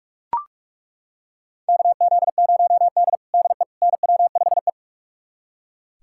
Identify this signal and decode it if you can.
Morse code, heard as a keyed tone, about 42 wpm, 700 Hz. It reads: KZ0D DE NW5E
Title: USA Callsign DE Callsign - 4 Characters - 45wpm